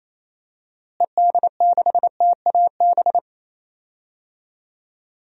Morse code, heard as a keyed tone, about 28 wpm, 700 Hz. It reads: ED6TAB
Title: DXCC-100 Call Signs - 6 Characters - Repeated 3x - 28wpm